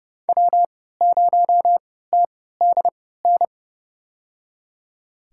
Morse code, heard as a keyed tone, about 29 wpm, spaced as for 17 wpm, 700 Hz. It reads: W0TDN
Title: USA Call Signs - 5 Characters - 3x Character Spacing - 30wpm